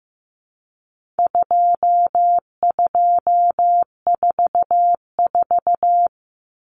Morse code, heard as a keyed tone, about 15 wpm, 700 Hz. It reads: 2244
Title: Common Cut Numbers - 4-Digit 15wpm